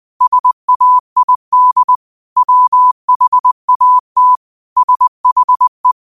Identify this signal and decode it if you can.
Morse code, heard as a keyed tone, about 20 wpm, 1000 Hz. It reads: SAID WHAT SHE
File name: qotdfiles / 20wpm